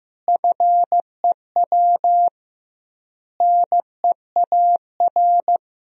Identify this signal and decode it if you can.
Morse code, heard as a keyed tone, about 16 wpm, 700 Hz. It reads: FEW NEAR
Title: 2x Word-Spacing - Sentences from Top 300 Words - 15wpm